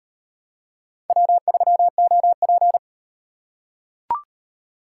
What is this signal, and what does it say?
Morse code, heard as a keyed tone, about 36 wpm, 700 Hz. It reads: W3OP